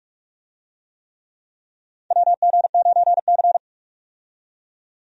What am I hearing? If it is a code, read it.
Morse code, heard as a keyed tone, about 43 wpm, 700 Hz. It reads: WG9C
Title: USA Call Signs - 4 Characters - 45wpm